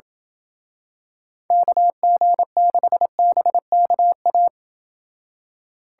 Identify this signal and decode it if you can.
Morse code, heard as a keyed tone, about 28 wpm, 700 Hz. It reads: KG6BKA